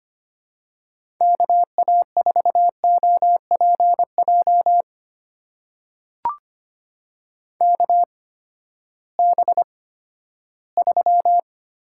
Morse code, heard as a keyed tone, about 25 wpm, spaced as for 14 wpm, 700 Hz. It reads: KA4OPJ KB3
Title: USA Call Signs - 8x Character Spacing - 25wpm